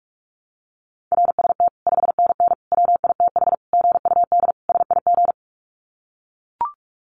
Morse code, heard as a keyed tone, about 44 wpm, 700 Hz. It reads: RST 5NN WITH GUD SIG
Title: Common QSO Phrases 45wpm